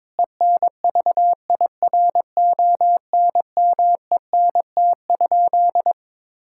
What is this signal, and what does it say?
Morse code, heard as a keyed tone, about 22 wpm, 700 Hz. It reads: ENVIRONMENT?
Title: Sentences from Top 1000 Words - Review Part 5 - 22wpm